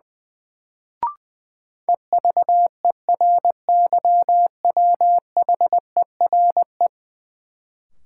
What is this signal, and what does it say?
Morse code, heard as a keyed tone, about 20 wpm, 700 Hz. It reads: EVERYWHERE
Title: Sentences from Top 1600 Words - New Word Drill - 20wpm